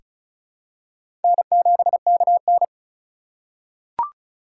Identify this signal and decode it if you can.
Morse code, heard as a keyed tone, about 36 wpm, 700 Hz. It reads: N7KN
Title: USA Call Signs - 4 Characters - 3x Character Spacing - 35wpm